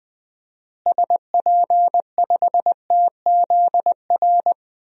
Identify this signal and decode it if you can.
Morse code, heard as a keyed tone, about 21 wpm, 700 Hz. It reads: SP5TZR
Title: DXCC-100 Call Signs - 6 Characters - Repeated 3x - 20wpm